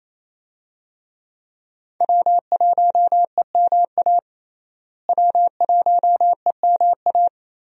Morse code, heard as a keyed tone, about 28 wpm, 700 Hz. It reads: W1EMA W1EMA
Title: USA Call Signs - 5 Characters - Repeated 3x - 28wpm